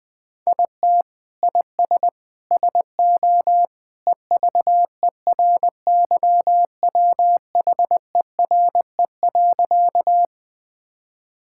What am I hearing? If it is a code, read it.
Morse code, heard as a keyed tone, about 21 wpm, 700 Hz. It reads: IT IS SO EVERYWHERE.
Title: Sentences from Top 1600 Words 20wpm